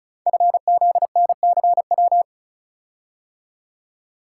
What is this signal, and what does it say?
Morse code, heard as a keyed tone, about 36 wpm, 700 Hz. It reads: FZNCW
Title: Code Groups - 5 Letters - 35wpm